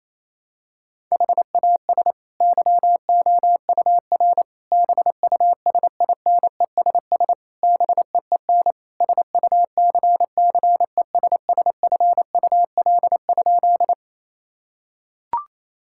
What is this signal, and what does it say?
Morse code, heard as a keyed tone, about 27 wpm, 700 Hz. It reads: HAS YOUR BUSINESS BEEN SUCCESSFUL?
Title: Sentences from Top 1000 Words - Review Part 6 - 28wpm